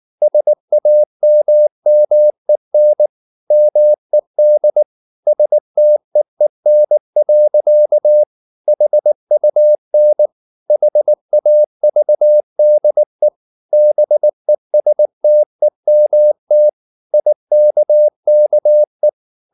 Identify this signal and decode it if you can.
Morse code, heard as a keyed tone, about 19 wpm, 600 Hz. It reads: SAMMEN MED STEEN. HUN HAVDE BESTEMT IKKE
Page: Ferie 19 wpm